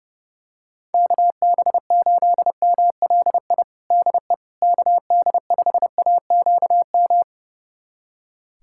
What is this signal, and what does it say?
Morse code, heard as a keyed tone, about 30 wpm, 700 Hz. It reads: KB8MLI DE KD5AQM
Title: USA Callsign DE Callsign 30wpm